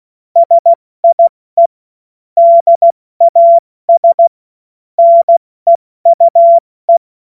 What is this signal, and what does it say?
Morse code, heard as a keyed tone, about 16 wpm, 680 Hz. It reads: SIE DAS NEUE
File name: Morsekurs / Witze / Witz 14.November 2023 70 F80